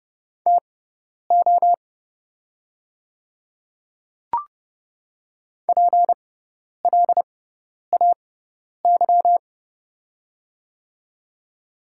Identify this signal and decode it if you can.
Morse code, heard as a keyed tone, about 30 wpm, spaced as for 10 wpm, 700 Hz. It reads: TO PLAY
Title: 6x Character Spacing - Top 300 Words - 30wpm